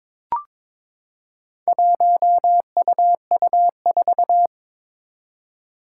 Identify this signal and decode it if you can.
Morse code, heard as a keyed tone, about 22 wpm, 700 Hz. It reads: 1UU4
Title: Four Challenging Letter-Number Combinations 22wpm